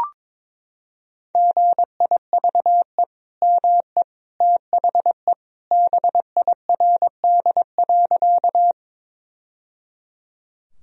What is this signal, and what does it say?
Morse code, heard as a keyed tone, about 22 wpm, 700 Hz. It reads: GIVE ME THE BIRD.
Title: Sentences from Top 1000 Words - Review Part 2 - 22wpm